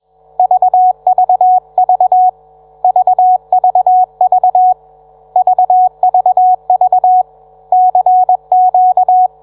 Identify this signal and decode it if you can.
Morse code, heard as a keyed tone, about 21 wpm, 740 Hz. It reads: VVV VVV VVV CQ